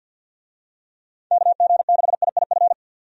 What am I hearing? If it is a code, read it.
Morse code, heard as a keyed tone, about 50 wpm, 700 Hz. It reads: KG6IIF
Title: USA Call Signs - 6 Characters - 50wpm